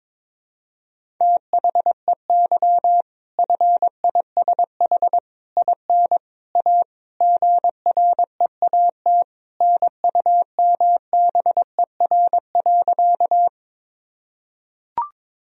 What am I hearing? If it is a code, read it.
Morse code, heard as a keyed tone, about 22 wpm, 700 Hz. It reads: THEY FISH IN A GREAT NUMBER.
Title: Sentences from Top 1000 Words - Review Part 1 - 22wpm